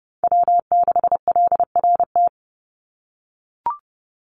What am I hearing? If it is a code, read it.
Morse code, heard as a keyed tone, about 30 wpm, 700 Hz. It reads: W6LRT
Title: USA Call Signs 30wpm